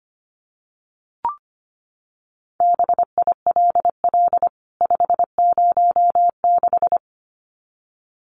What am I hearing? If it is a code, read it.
Morse code, heard as a keyed tone, about 25 wpm, 700 Hz. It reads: BILL 506
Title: CWT Exchange 25wpm